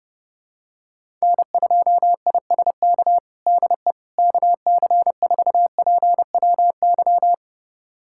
Morse code, heard as a keyed tone, about 32 wpm, 700 Hz. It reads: N2ISK DE KC4PWY
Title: USA Callsign DE Callsign 30wpm